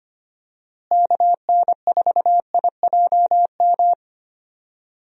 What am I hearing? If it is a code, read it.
Morse code, heard as a keyed tone, about 25 wpm, 700 Hz. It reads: KN4IJM
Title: USA Call Signs - 6 Characters - Repeated 2x - 25wpm